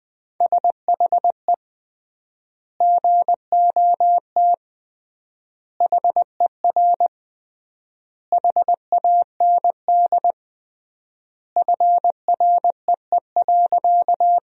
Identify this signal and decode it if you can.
Morse code, heard as a keyed tone, about 20 wpm, 700 Hz. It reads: SHE GOT HER HAND FREE.
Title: 3x Word-Spacing - Sentences from Top 500 Words - 20wpm